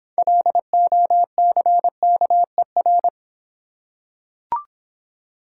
Morse code, heard as a keyed tone, about 26 wpm, 700 Hz. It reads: LOCKER